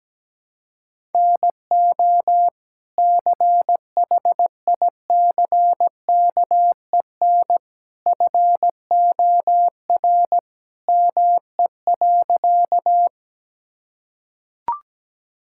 Morse code, heard as a keyed tone, about 17 wpm, 700 Hz. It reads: NO CHICKEN FOR ME.
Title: Sentences from Top 1000 Words - Review Part 5 - 17wpm